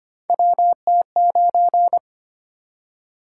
Morse code, heard as a keyed tone, about 25 wpm, 700 Hz. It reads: WT9
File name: Three Letter-Number Combinations - 25wpm